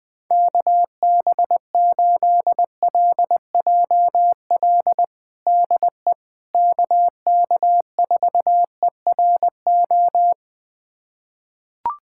Morse code, heard as a keyed tone, about 20 wpm, 700 Hz. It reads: KB8LJL DE KK4ERO
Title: USA Callsign DE Callsign 20wpm